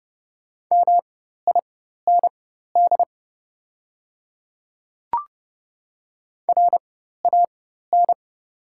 Morse code, heard as a keyed tone, about 29 wpm, spaced as for 14 wpm, 700 Hz. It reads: MIND RAN
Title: 4x Character Spacing - Top 500 Words - 30wpm